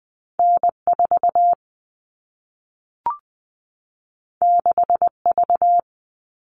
Morse code, heard as a keyed tone, about 20 wpm, 700 Hz. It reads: N4 6V
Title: All Cut Numbers - 2-Digit 20wpm